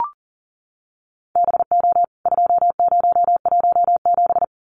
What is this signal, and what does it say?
Morse code, heard as a keyed tone, about 40 wpm, 700 Hz. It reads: BO 2017
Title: CWT Exchange 40wpm